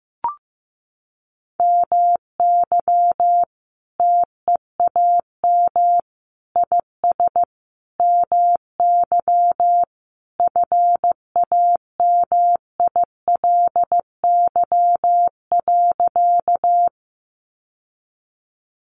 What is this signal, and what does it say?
Morse code, heard as a keyed tone, about 15 wpm, 700 Hz. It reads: MY TEAM IS MY FAMILY.
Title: Sentences from Top 1500 Words 15wpm